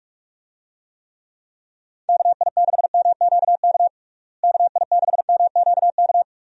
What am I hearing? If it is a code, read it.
Morse code, heard as a keyed tone, about 41 wpm, 700 Hz. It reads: KI6MQK KI6MQK
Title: USA Call Signs - 6 Characters - Repeated 3x - 45wpm